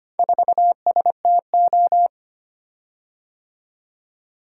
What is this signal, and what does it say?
Morse code, heard as a keyed tone, about 25 wpm, 700 Hz. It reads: 4STO
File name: practice / 25wpm / Four Letter-Number Combinations - 25wpm